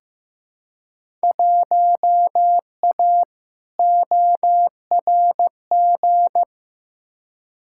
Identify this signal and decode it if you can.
Morse code, heard as a keyed tone, about 15 wpm, 700 Hz. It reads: 1A ORG
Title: Field Day Exchanges - 15wpm